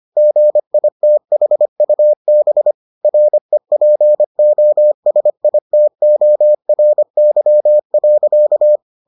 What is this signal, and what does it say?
Morse code, heard as a keyed tone, about 25 wpm, 600 Hz. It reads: GITHUB REPOSITORY.